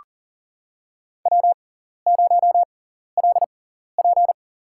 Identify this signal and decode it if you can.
Morse code, heard as a keyed tone, about 40 wpm, spaced as for 14 wpm, 700 Hz. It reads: W0LP
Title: USA Call Signs - 6x Character Spacing - 40wpm